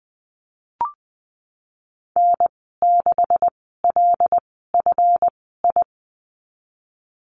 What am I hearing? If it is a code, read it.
Morse code, heard as a keyed tone, about 20 wpm, spaced as for 15 wpm, 700 Hz. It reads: N6LFI
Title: 193 Lesson 31g - Letter v - Call Signs 20wpm